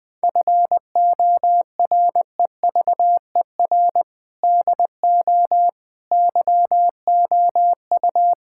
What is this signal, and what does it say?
Morse code, heard as a keyed tone, about 19 wpm, 700 Hz. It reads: FOREVER DO YOU
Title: Sentences from Top 1600 Words 20wpm